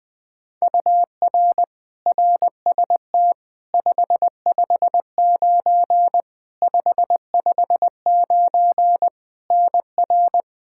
Morse code, heard as a keyed tone, about 20 wpm, 700 Hz. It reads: UR RST 559 559 NR
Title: Practice QSO - Exchange 1 - Beginning - 20wpm